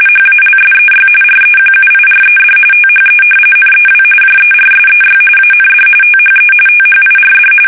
RTTY, about 45.45 baud. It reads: HALSEY REQUIRED THE NEW JERSEY AND THE THIRD
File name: fsk850 / battleshipnj